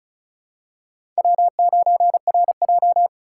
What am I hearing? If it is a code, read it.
Morse code, heard as a keyed tone, about 34 wpm, 700 Hz. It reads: W9RJ
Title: USA Call Signs - 3x Character Spacing - 35wpm